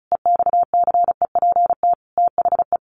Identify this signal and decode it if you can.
Morse code, heard as a keyed tone, about 34 wpm, 700 Hz. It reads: EXCEPT THE
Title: A Sound Of Thunder 35wpm